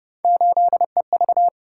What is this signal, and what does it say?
Morse code, heard as a keyed tone, about 31 wpm, 700 Hz. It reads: 8EV
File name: Speed-Racing - Three Letter-Number Combinations - 25wpm